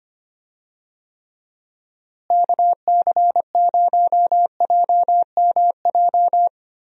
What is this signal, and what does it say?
Morse code, heard as a keyed tone, about 25 wpm, 700 Hz. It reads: KC0JMJ
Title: USA Call Signs - 6 Characters - Repeated 3x - 25wpm